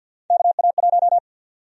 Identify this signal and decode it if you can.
Morse code, heard as a keyed tone, about 52 wpm, 700 Hz. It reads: KA1
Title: USA Call Sign Prefixes - 50wpm